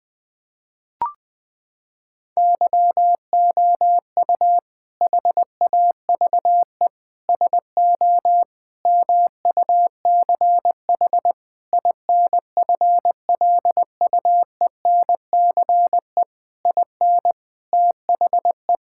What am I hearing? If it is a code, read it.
Morse code, heard as a keyed tone, about 20 wpm, 700 Hz. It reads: YOU HAVE SO MUCH INFLUENCE IN THE
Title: Sentences from Top 1600 Words 20wpm